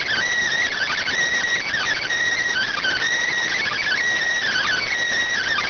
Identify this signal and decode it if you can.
Emergency Alert System (SAME) tones, heard as harsh buzzing bursts. It is heard from 0.7 to 4.9 s.